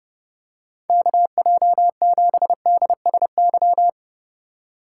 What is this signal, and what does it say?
Morse code, heard as a keyed tone, about 32 wpm, 700 Hz. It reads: KJ7DSY
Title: USA Call Signs - 6x Character Spacing - 30wpm